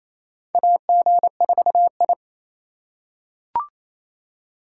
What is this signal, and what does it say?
Morse code, heard as a keyed tone, about 28 wpm, 700 Hz. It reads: AG4I
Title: USA Call Signs - 2x Character Spacing - 28wpm